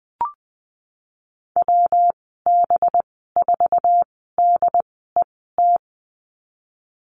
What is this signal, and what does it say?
Morse code, heard as a keyed tone, about 20 wpm, spaced as for 15 wpm, 700 Hz. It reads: WB4DET
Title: 186 Lesson 30g - Letter b - Call Signs 20wpm